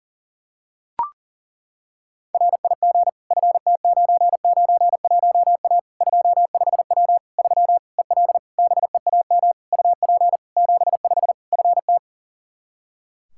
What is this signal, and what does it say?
Morse code, heard as a keyed tone, about 40 wpm, 700 Hz. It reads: RIG FT991A 25W 3 EL BEAM UP 75 FT
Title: K1USN SST QSO Practice 40wpm